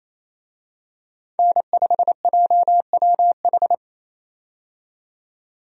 Morse code, heard as a keyed tone, about 28 wpm, 700 Hz. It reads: N5JWH
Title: USA Call Signs - 5 Characters - 28wpm